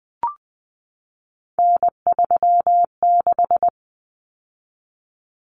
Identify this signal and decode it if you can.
Morse code, heard as a keyed tone, about 20 wpm, 700 Hz. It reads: N36
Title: Common Cut Numbers - 3-Digit 20wpm